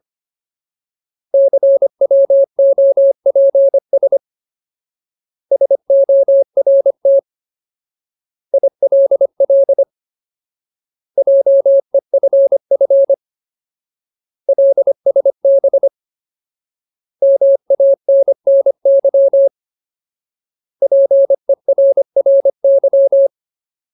Morse code, heard as a keyed tone, about 25 wpm, 550 Hz. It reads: CWOPS SORT ILL JEFF LSB MANNY PERRY
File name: Ultimate Word List / 25WPM Ultimate